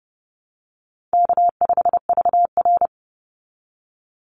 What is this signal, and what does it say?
Morse code, heard as a keyed tone, about 30 wpm, 700 Hz. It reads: K5VR
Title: Call Signs Repeated 3x - 30wpm